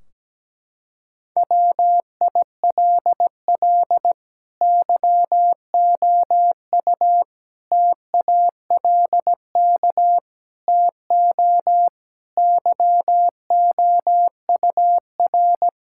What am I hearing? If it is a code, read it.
Morse code, heard as a keyed tone, about 17 wpm, 700 Hz. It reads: WILL YOU TALK TO YOUR
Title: Sentences from Top 1600 Words 17wpm